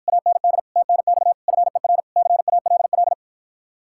Morse code, heard as a keyed tone, about 54 wpm, 700 Hz. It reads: AND TNX FER CALL